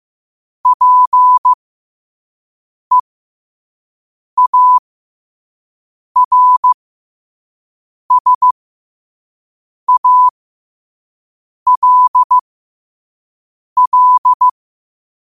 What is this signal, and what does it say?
Morse code, heard as a keyed tone, about 15 wpm, spaced as for 5 wpm, 1000 Hz. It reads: PEARSALL
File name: qotdfiles / 5wpm